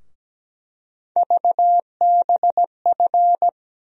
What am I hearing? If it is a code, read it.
Morse code, heard as a keyed tone, about 17 wpm, 700 Hz. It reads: VBF
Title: USA Call Sign Postfixes - 3 Letters 17wpm